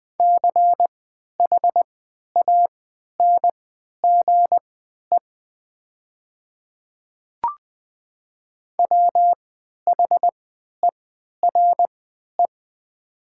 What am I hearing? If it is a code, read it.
Morse code, heard as a keyed tone, about 20 wpm, spaced as for 11 wpm, 700 Hz. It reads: CHANGE WHERE
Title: 3x Character Spacing - Top 300 Words - 20wpm